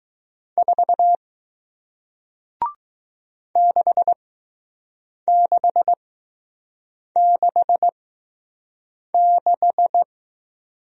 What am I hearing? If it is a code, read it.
Morse code, heard as a keyed tone, about 21 wpm, spaced as for 6 wpm, 700 Hz. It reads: 4 6666